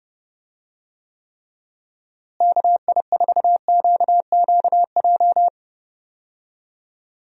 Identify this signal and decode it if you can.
Morse code, heard as a keyed tone, about 30 wpm, 700 Hz. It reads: KI4QQJ
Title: USA Call Signs - 6 Characters - Repeated 3x - 30wpm